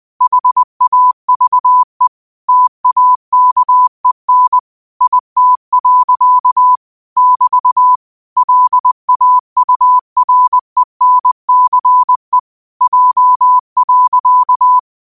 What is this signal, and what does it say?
Morse code, heard as a keyed tone, about 20 wpm, 1000 Hz. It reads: HAVE TAKEN IT. <BT> LAURENCE J.
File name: qotdfiles / 20wpm